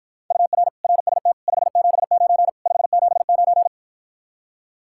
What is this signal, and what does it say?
Morse code, heard as a keyed tone, about 54 wpm, 700 Hz. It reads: UR RST 579 579